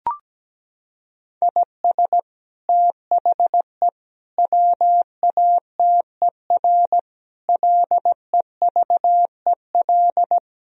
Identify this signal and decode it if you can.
Morse code, heard as a keyed tone, about 17 wpm, 700 Hz. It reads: IS THE WATER LEVEL
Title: Sentences from Top 1000 Words - Review Part 4 - 17wpm